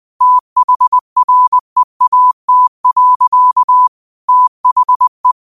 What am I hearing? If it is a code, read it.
Morse code, heard as a keyed tone, about 20 wpm, 1000 Hz. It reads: THREAT. THE